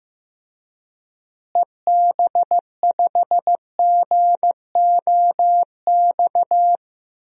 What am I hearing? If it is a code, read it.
Morse code, heard as a keyed tone, about 15 wpm, 700 Hz. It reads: EB5GOX
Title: DXCC-100 Call Signs - 6 Characters - Repeated 3x - 15wpm